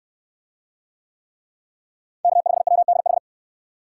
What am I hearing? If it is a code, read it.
Morse code, heard as a keyed tone, about 68 wpm, 700 Hz. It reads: K5WDH